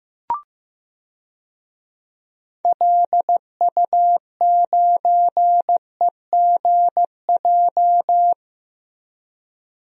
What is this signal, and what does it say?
Morse code, heard as a keyed tone, about 15 wpm, 700 Hz. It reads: LU9EGJ
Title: DXCC-100 Call Signs - 6 Characters - Repeated 2x - 15wpm